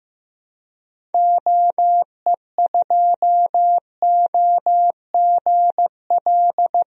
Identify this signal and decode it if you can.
Morse code, heard as a keyed tone, about 15 wpm, 700 Hz. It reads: OE2OGL
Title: DXCC-100 Call Signs - 6 Characters - Repeated 3x - 15wpm